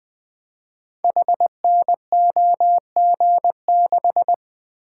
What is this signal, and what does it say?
Morse code, heard as a keyed tone, about 20 wpm, 700 Hz. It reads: HNOG6
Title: Code Groups - 5 Letter:Numbers - 20wpm